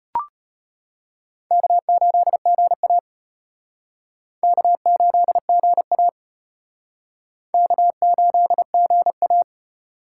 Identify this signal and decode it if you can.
Morse code, heard as a keyed tone, about 34 wpm, 700 Hz. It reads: K8GA K8GA K8GA